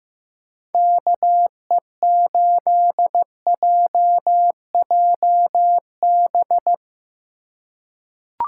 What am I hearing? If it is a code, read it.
Morse code, heard as a keyed tone, about 15 wpm, 700 Hz. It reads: KE8JJB
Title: USA Call Signs - 6 Characters - 3x Character Spacing - 15wpm